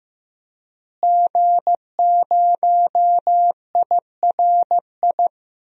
Morse code, heard as a keyed tone, about 16 wpm, 700 Hz. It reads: G0IRI
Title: DXCC-100 Call Signs - 5 Characters - Repeated 3x - 15wpm